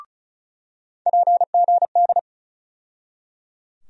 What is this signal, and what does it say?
Morse code, heard as a keyed tone, about 36 wpm, 700 Hz. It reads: PGD
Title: USA Call Sign Postfixes - 3 Letters 35wpm